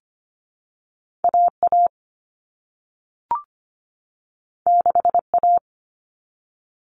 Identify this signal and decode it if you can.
Morse code, heard as a keyed tone, about 25 wpm, 700 Hz. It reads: AA 6A
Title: All Cut Numbers - 2-Digit 25wpm